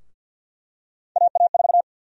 Morse code, heard as a keyed tone, about 50 wpm, 700 Hz. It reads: AA4
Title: USA Call Sign Prefixes - 50wpm